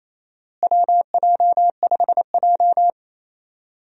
Morse code, heard as a keyed tone, about 28 wpm, 700 Hz. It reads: WJ5J
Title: USA Call Signs - 4 Characters - Repeated 2x - 28wpm